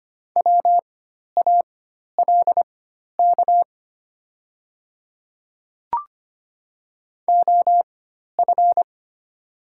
Morse code, heard as a keyed tone, about 25 wpm, spaced as for 12 wpm, 700 Hz. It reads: WALK OF
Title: 4x Character Spacing - Top 500 Words - 25wpm